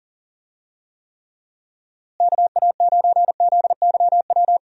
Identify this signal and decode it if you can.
Morse code, heard as a keyed tone, about 39 wpm, 700 Hz. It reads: KA9ZYW
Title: USA Call Signs - 6 Characters - Repeated 3x - 40wpm